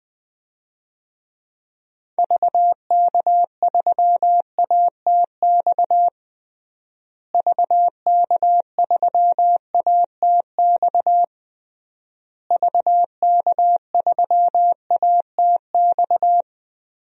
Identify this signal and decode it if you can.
Morse code, heard as a keyed tone, about 20 wpm, 700 Hz. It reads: VK3ATX VK3ATX VK3ATX
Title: DXCC-100 Call Signs - 6 Characters - Repeated 3x - 20wpm